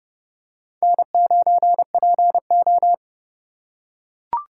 Morse code, heard as a keyed tone, about 30 wpm, 700 Hz. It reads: N9PO